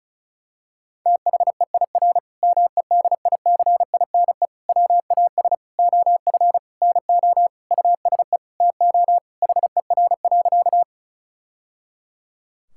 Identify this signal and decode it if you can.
Morse code, heard as a keyed tone, about 36 wpm, 700 Hz. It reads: THEIR MEDICINE WAS OF NO USE TO HER.
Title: Sentences from Top 1000 Words - Review Part 6 - 35wpm